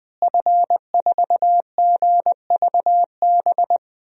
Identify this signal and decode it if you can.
Morse code, heard as a keyed tone, about 21 wpm, 700 Hz. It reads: F4GVB
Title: DXCC-100 Call Signs - 5 Characters - Repeated 3x - 20wpm